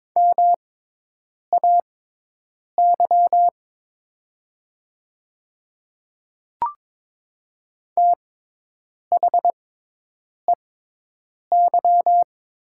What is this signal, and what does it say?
Morse code, heard as a keyed tone, about 22 wpm, spaced as for 8 wpm, 700 Hz. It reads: MAY THEY
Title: 6x Character Spacing - Top 100 Words - 22wpm